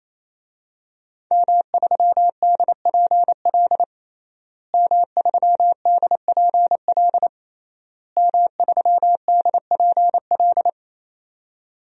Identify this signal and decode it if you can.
Morse code, heard as a keyed tone, about 28 wpm, 700 Hz. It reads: M3DPL M3DPL M3DPL
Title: DXCC-100 Call Signs - 5 Characters - Repeated 3x - 28wpm